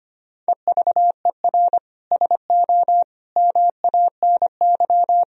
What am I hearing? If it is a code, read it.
Morse code, heard as a keyed tone, about 25 wpm, 700 Hz. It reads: EVER SO MANY
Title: Sentences from Top 1000 Words - Review Part 3 - 25wpm